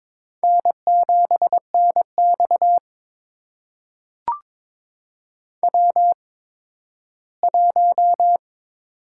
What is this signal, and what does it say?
Morse code, heard as a keyed tone, about 22 wpm, spaced as for 13 wpm, 700 Hz. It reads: N7NX W1
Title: USA Call Signs - 4 Characters - 8x Character Spacing - 22wpm